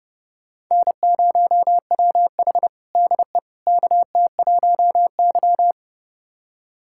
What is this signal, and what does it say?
Morse code, heard as a keyed tone, about 30 wpm, 700 Hz. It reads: N0WH DE KT1Y
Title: USA Callsign DE Callsign - 4 Characters - 30wpm